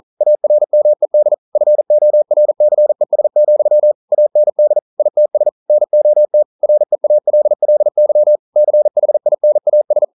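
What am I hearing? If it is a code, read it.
Morse code, heard as a keyed tone, about 42 wpm, 600 Hz. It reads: ARMED FORCES, AND ITS NOT REALLY CHINAS
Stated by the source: Morse code practice at 41 wpm, from CW med Gnister.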